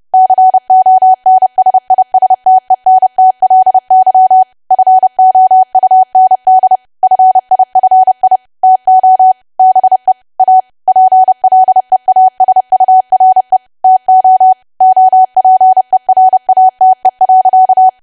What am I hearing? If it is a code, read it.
Morse code, heard as a keyed tone, about 30 wpm, 750 Hz. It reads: CONSISTENTLY FOUND FIFI TO BE A PLEASURE TO OPERATE.
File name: cwa / W1AW-Level3 / 30WPM